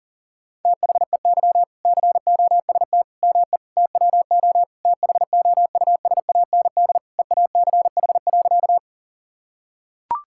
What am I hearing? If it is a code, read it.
Morse code, heard as a keyed tone, about 40 wpm, 700 Hz. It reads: THEY COST ME TWO THOUSAND EACH.
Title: Sentences from Top 1000 Words - Review Part 5 - 40wpm